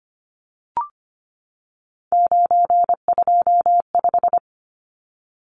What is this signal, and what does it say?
Morse code, heard as a keyed tone, about 25 wpm, 690 Hz. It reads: 925
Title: Numbers - 3-Digit 25wpm